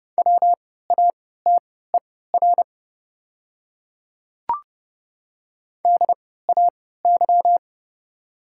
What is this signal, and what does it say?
Morse code, heard as a keyed tone, about 30 wpm, spaced as for 17 wpm, 700 Hz. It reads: WATER DAY
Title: 3x Character Spacing - Top 100 Words - 30wpm